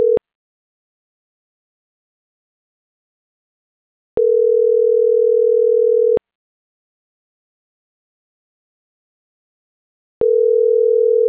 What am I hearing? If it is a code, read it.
Ringback tone (North America), 440 + 480 Hz.